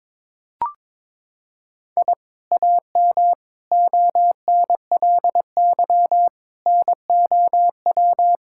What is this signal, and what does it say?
Morse code, heard as a keyed tone, about 22 wpm, 700 Hz. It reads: I AM ONLY NOW
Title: Sentences from Top 1000 Words - Review Part 5 - 22wpm